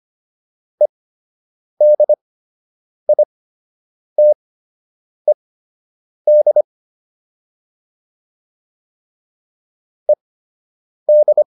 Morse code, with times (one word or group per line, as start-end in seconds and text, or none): 0.8-6.6 EDITED
10.1-11.4 ED